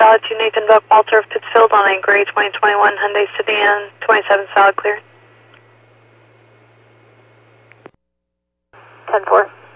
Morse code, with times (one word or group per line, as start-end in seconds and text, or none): none